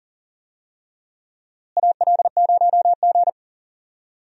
Morse code, heard as a keyed tone, about 40 wpm, 700 Hz. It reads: AL0G
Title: USA Call Signs - 4 Characters - Repeated 2x - 40wpm